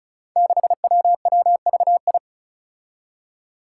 Morse code, heard as a keyed tone, about 36 wpm, 700 Hz. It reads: 6WWVI
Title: Five Challenging Letter-Number Combinations 35wpm